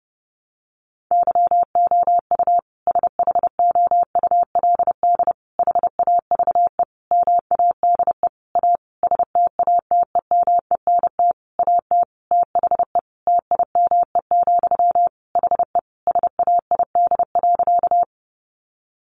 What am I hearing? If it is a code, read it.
Morse code, heard as a keyed tone, about 30 wpm, 700 Hz. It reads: YOU SHOULD HAVE MADE A STATEMENT AT THE TIME, HE SAID.
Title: Sentences from Top 1200 Words 30wpm